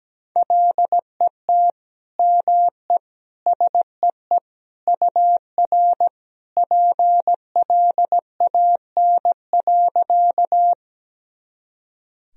Morse code, heard as a keyed tone, about 17 wpm, 700 Hz. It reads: LET ME SEE UR PLAN.
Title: Sentences from Top 500 Words and Abbreviations 17wpm